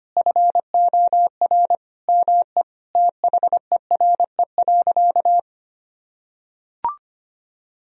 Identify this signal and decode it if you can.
Morse code, heard as a keyed tone, about 25 wpm, 700 Hz. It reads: FOR ME THERE.
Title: Sentences from Top 1000 Words - Review Part 4 - 25wpm